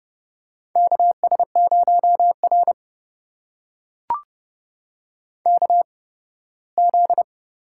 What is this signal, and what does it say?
Morse code, heard as a keyed tone, about 31 wpm, spaced as for 18 wpm, 700 Hz. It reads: KS0R KZ
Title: USA Call Signs - 4 Characters - 8x Character Spacing - 30wpm